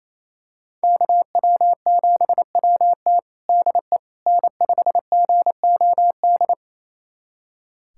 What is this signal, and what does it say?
Morse code, heard as a keyed tone, about 28 wpm, 700 Hz. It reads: KW7WT DE N5GOD
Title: USA Callsign DE Callsign - 5 Characters - 28wpm